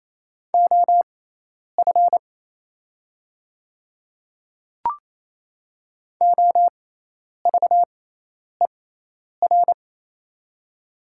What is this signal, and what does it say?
Morse code, heard as a keyed tone, about 28 wpm, spaced as for 10 wpm, 700 Hz. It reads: OF OVER
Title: 6x Character Spacing - Top 100 Words - 28wpm